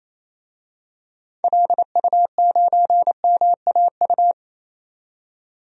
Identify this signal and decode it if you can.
Morse code, heard as a keyed tone, about 28 wpm, 700 Hz. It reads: LU9MAU
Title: DXCC-100 Call Signs - 6 Characters - 28wpm